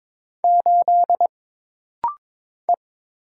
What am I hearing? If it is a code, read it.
Morse code, heard as a keyed tone, about 22 wpm, 700 Hz. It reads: 8 E
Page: This rapid-fire timing is geared to push your ICR (Instant Character Recognition) skills.